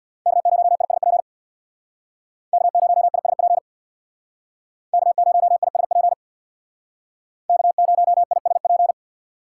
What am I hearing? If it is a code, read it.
Morse code, heard as a keyed tone, about 64 wpm, 700 Hz. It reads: K0ISP K0ISP K0ISP K0ISP